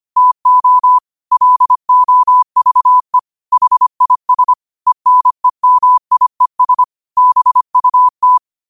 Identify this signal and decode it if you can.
Morse code, heard as a keyed tone, about 25 wpm, 1000 Hz. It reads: TO LOVE HIS ENEMIES BUT
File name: qotdfiles / 25wpm